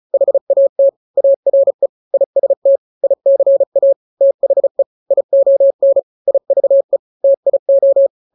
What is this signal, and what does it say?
Morse code, heard as a keyed tone, about 34 wpm, 550 Hz. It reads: HAT ARE IST ICA THE ION IVE TIO
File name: 3-grams 35 30 WPM